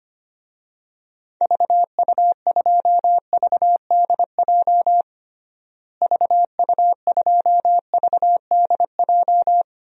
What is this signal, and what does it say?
Morse code, heard as a keyed tone, about 25 wpm, 700 Hz. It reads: VU2VDJ VU2VDJ
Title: DXCC-100 Call Signs - 6 Characters - Repeated 2x - 25wpm